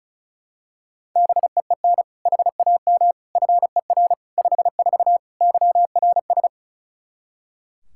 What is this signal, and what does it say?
Morse code, heard as a keyed tone, about 34 wpm, 700 Hz. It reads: BEEN HAM FER 54 YRS
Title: K1USN SST QSO Practice 35wpm